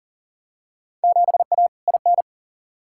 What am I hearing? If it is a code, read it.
Morse code, heard as a keyed tone, about 40 wpm, 700 Hz. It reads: 7A IN
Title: Field Day Exchanges Encore - 40wpm